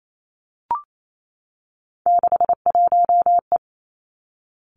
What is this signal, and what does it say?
Morse code, heard as a keyed tone, about 28 wpm, 700 Hz. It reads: 61E
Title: Three Challenging Letter-Number Combinations 28wpm